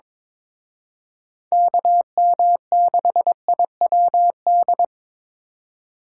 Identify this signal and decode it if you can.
Morse code, heard as a keyed tone, about 26 wpm, 700 Hz. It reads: KM6IWD
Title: Speed Racing - USA 2x3 Call Signs 22wpm